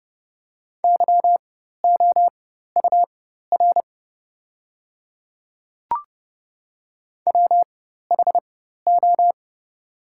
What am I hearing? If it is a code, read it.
Morse code, heard as a keyed tone, about 30 wpm, spaced as for 14 wpm, 700 Hz. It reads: YOUR WHO
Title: 4x Character Spacing - Top 100 Words - 30wpm